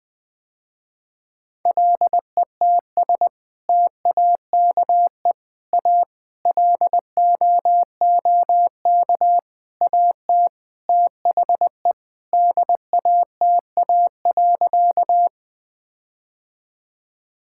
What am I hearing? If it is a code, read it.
Morse code, heard as a keyed tone, about 20 wpm, 700 Hz. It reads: LETS TAKE A LOOK AT THE DATA.
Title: Sentences with 6-Word Phrases - Spoken Beginning - 20wpm